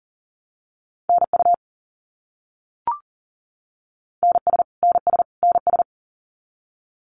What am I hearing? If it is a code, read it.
Morse code, heard as a keyed tone, about 40 wpm, 700 Hz. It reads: NU NS NS NS
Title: Canadian Provinces and Territories - Repeated 3x - 40wpm